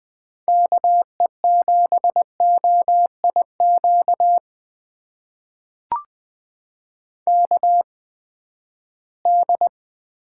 Morse code, heard as a keyed tone, about 21 wpm, spaced as for 14 wpm, 700 Hz. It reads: KE7OIQ KD
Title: USA Call Signs - 6 Characters - 8x Character Spacing - 20wpm